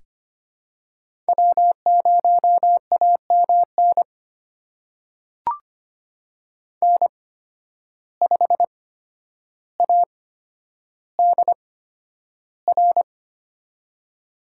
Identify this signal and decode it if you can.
Morse code, heard as a keyed tone, about 25 wpm, spaced as for 11 wpm, 700 Hz. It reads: W0AMN N5ADR
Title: USA Call Signs - 8x Character Spacing - 25wpm